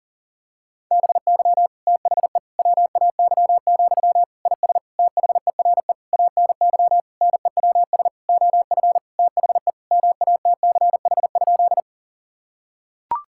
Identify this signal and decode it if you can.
Morse code, heard as a keyed tone, about 40 wpm, 700 Hz. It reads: BY THE WAY, IS THERE ANY NEWS OF THE MATCH?
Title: Sentences from Top 1600 Words 40wpm